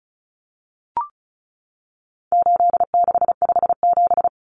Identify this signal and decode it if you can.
Morse code, heard as a keyed tone, about 34 wpm, 700 Hz. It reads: 8657
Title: Common Cut Numbers - 4-Digit 35wpm